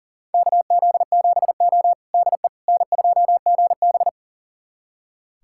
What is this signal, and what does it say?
Morse code, heard as a keyed tone, about 43 wpm, 690 Hz. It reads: KZ7O DE N2GB
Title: USA Callsign DE Callsign - 4 Characters - 40wpm